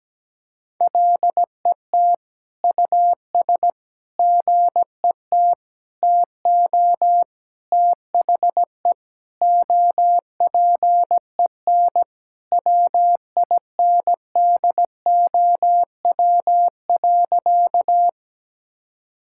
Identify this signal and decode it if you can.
Morse code, heard as a keyed tone, about 17 wpm, 700 Hz. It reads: LET US GET TO THE OPEN WINDOW.
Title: Sentences from Top 1600 Words 17wpm